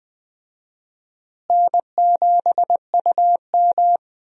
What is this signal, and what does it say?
Morse code, heard as a keyed tone, about 20 wpm, 700 Hz. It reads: N7UM
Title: USA Call Signs - 4 Characters - Repeated 3x - 20wpm